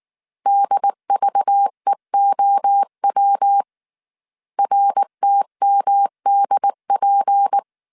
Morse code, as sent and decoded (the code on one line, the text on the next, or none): -.. ...- . --- .-- / .-. - -- -.. .--.
DVEOW RTMDP